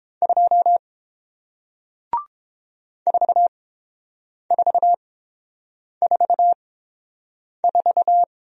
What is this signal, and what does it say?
Morse code, heard as a keyed tone, about 30 wpm, spaced as for 7 wpm, 700 Hz. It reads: 2 4444